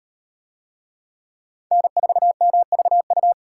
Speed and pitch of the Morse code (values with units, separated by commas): 38 wpm, 700 Hz